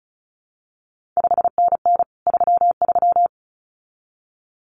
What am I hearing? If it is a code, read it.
Morse code, heard as a keyed tone, about 36 wpm, 700 Hz. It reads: 5NN 33
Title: CQ WW DX Contest Exchange Encore - 35wpm